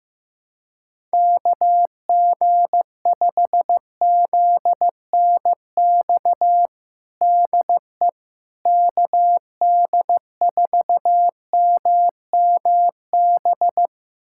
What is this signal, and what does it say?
Morse code, heard as a keyed tone, about 15 wpm, 700 Hz. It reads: KG5ZNX DE KD4MMB
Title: USA Callsign DE Callsign 15wpm